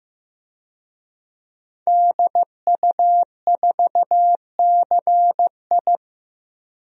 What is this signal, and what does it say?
Morse code, heard as a keyed tone, about 15 wpm, 700 Hz. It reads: DU4CI
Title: DXCC-100 Call Signs - 5 Characters - Repeated 2x - 15wpm